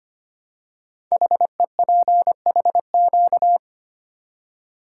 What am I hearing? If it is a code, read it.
Morse code, heard as a keyed tone, about 25 wpm, 700 Hz. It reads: HEPHQ
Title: Code Groups - 5 Letters - 25wpm